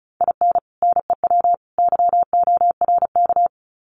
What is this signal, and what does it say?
Morse code, heard as a keyed tone, about 34 wpm, 700 Hz. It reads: IN NEW YORK
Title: Sets of 3 Words 35wpm